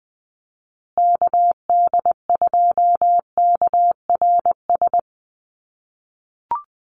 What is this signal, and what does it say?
Morse code, heard as a keyed tone, about 20 wpm, 700 Hz. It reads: KD2KRS
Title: USA Call Signs Repeated 2x 20wpm